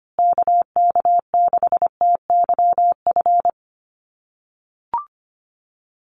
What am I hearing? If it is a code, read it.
Morse code, heard as a keyed tone, about 25 wpm, 700 Hz. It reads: KK6TYF
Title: USA Call Signs Repeated 2x 25wpm